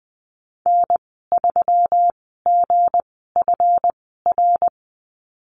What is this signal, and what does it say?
Morse code, heard as a keyed tone, about 20 wpm, spaced as for 15 wpm, 700 Hz. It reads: N3GFR
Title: 179 Lesson 29g - Slash - Call Signs 20wpm